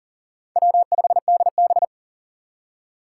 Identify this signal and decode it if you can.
Morse code, heard as a keyed tone, about 40 wpm, 700 Hz. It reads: W5DB
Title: USA Call Signs - 4 Characters - 3x Character Spacing - 40wpm